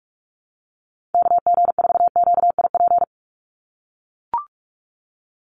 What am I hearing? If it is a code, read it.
Morse code, heard as a keyed tone, about 44 wpm, 700 Hz. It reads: KG4QIP
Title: USA Call Signs 45wpm